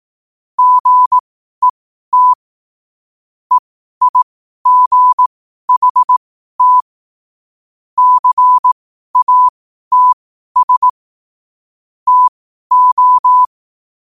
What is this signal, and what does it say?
Morse code, heard as a keyed tone, about 18 wpm, spaced as for 12 wpm, 1000 Hz. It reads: GET EIGHT CATS TO
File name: qotdfiles / 12wpm